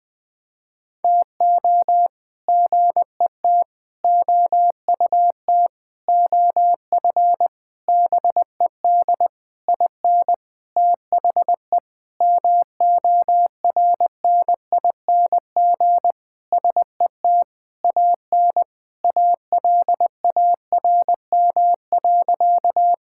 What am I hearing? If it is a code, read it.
Morse code, heard as a keyed tone, about 20 wpm, 700 Hz. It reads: TO GET OUT OF BED IN THE MORNING SET AN ALARM.
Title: Sentences with 8-Word Phrases - 20wpm